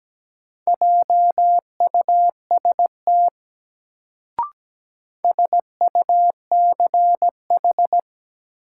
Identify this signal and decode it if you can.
Morse code, heard as a keyed tone, about 17 wpm, 700 Hz. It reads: JUST SUCH
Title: Sentences from Top 200 Words - New Word Drill - 17wpm